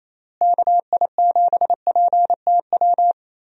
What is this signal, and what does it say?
Morse code, heard as a keyed tone, about 28 wpm, 700 Hz. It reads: KI7PTW
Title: USA Call Signs - 6 Characters - Repeated 2x - 28wpm